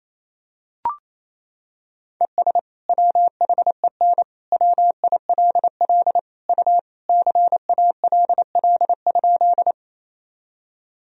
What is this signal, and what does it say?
Morse code, heard as a keyed tone, about 28 wpm, 700 Hz. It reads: ES WHEN WILL U CALL?
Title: Sentences from Top 100 Words and Abbreviations 28wpm